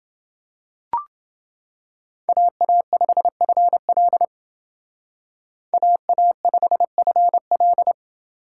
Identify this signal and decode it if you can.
Morse code, heard as a keyed tone, about 29 wpm, 700 Hz. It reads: AA5FL AA5FL